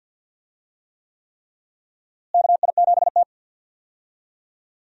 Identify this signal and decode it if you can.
Morse code, heard as a keyed tone, about 52 wpm, 700 Hz. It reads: KI7T
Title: USA Call Signs - 4 Characters - Repeated 3x - 50wpm